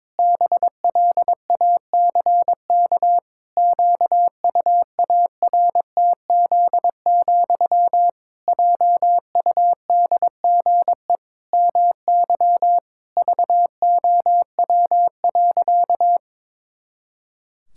Morse code, heard as a keyed tone, about 22 wpm, 700 Hz. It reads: BLACK QUARTZ, JUDGE MY VOW.
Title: Pangram Challenge v2 - 22wpm